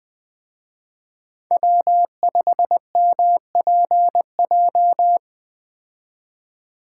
Morse code, heard as a keyed tone, about 20 wpm, 700 Hz. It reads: W5MPJ
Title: Speed Racing - USA 1x3 Call Signs 17wpm